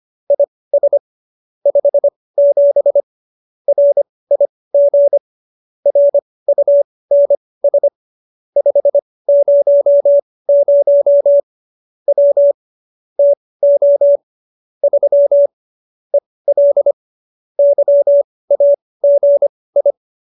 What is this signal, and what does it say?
Morse code, heard as a keyed tone, about 25 wpm, spaced as for 18 wpm, 570 Hz. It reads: IS 57 RIG RUNS 500 W TO 3 EL YAGI
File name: qso201b 18